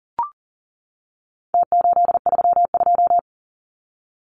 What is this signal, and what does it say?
Morse code, heard as a keyed tone, about 40 wpm, 700 Hz. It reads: T832
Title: Numbers - Contest Style - 4-Digit 40wpm